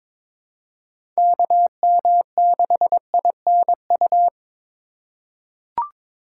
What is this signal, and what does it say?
Morse code, heard as a keyed tone, about 22 wpm, 700 Hz. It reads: KM6INU
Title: USA Call Signs - 3x Character Spacing - 22wpm